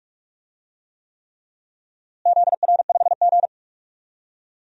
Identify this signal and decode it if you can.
Morse code, heard as a keyed tone, about 46 wpm, 700 Hz. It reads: ZR5G
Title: DXCC-100 Call Signs - 4 Characters - Repeated 3x - 45wpm